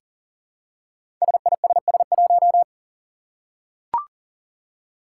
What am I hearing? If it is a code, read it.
Morse code, heard as a keyed tone, about 40 wpm, 700 Hz. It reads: SISS1
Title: Five Challenging Letter-Number Combinations 40wpm